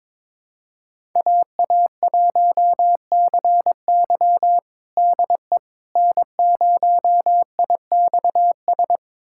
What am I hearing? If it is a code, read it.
Morse code, heard as a keyed tone, about 22 wpm, 700 Hz. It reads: AA1CY DE N0IXS
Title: USA Callsign DE Callsign - 5 Characters - 22wpm